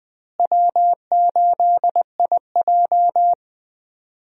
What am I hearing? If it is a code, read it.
Morse code, heard as a keyed tone, about 20 wpm, 700 Hz. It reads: W8IJ
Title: USA Call Signs - 4 Characters - Repeated 3x - 20wpm